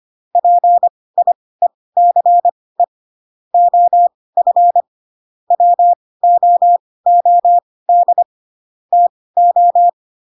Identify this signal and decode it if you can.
Morse code, heard as a keyed tone, about 25 wpm, spaced as for 18 wpm, 700 Hz. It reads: PIECE OF WOOD TO
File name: Pinocchio / Pinocchio ch2 12 25 ik1hge